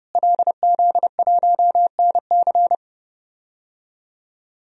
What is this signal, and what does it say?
Morse code, heard as a keyed tone, about 30 wpm, 700 Hz. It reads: LZ1NC
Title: DXCC-100 Call Signs - 5 Characters - Repeated 2x - 30wpm